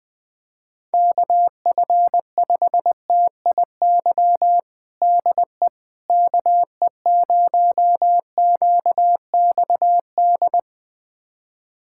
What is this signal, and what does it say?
Morse code, heard as a keyed tone, about 20 wpm, 700 Hz. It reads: KF5TIY DE KE0QXD
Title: USA Callsign DE Callsign - 6 Characters - 20wpm